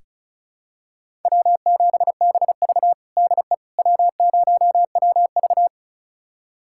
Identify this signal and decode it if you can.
Morse code, heard as a keyed tone, about 36 wpm, 700 Hz. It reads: W7BV DE W0WV
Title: USA Callsign DE Callsign - 4 Characters - 35wpm